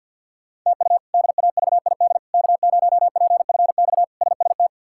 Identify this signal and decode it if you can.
Morse code, heard as a keyed tone, about 46 wpm, 700 Hz. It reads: TU DAVID K0PFX SST
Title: K1USN SST QSO Practice - 50wpm